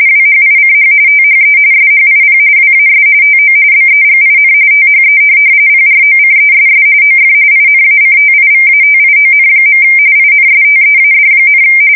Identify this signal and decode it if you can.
RTTY, about 45.45 baud. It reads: WHICH THE SHIP WAS TO BEGIN THE PROCESS OF INACTIVATION. INSTEAD,
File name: fsk170 / USS-Missouri